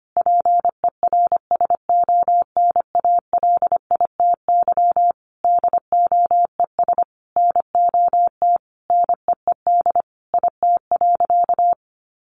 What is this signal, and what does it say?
Morse code, heard as a keyed tone, about 25 wpm, 700 Hz. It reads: PERSONALITY DOES NOT NEED IT.
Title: Sentences from Top 1400 Words 25wpm